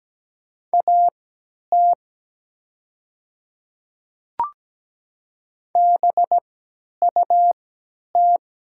Morse code, heard as a keyed tone, about 17 wpm, spaced as for 10 wpm, 690 Hz. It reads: AT BUT
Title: 3x Character Spacing - Top 100 Words - 17wpm